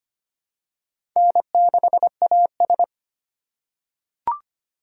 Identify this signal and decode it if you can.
Morse code, heard as a keyed tone, about 25 wpm, 700 Hz. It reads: N6AS
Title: USA Call Signs - 4 Characters - 4x Character Spacing - 25wpm